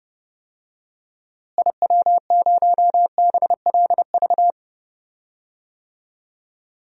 Morse code, heard as a keyed tone, about 30 wpm, 700 Hz. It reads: IW0BLV
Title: DXCC-100 Call Signs - 6 Characters - Repeated 2x - 30wpm